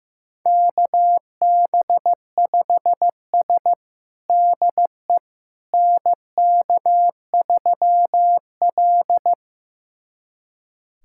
Morse code, heard as a keyed tone, about 15 wpm, 700 Hz. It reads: KB5S DE NK3L
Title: USA Callsign DE Callsign - 4 Characters - 15wpm